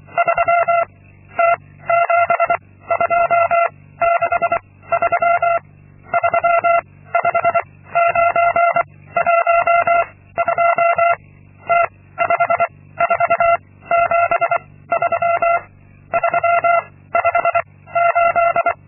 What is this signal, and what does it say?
Morse code, heard as a keyed tone, about 24 wpm, 680 Hz. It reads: 3 T 7 2 6 3 3 5 9 1 2 T 5 4 7 3 3 5 8
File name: Linksmakalnis-2